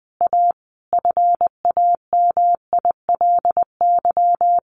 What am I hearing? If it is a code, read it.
Morse code, heard as a keyed tone, about 20 wpm, 700 Hz. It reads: A FAMILY
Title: Sentences from Top 1500 Words 20wpm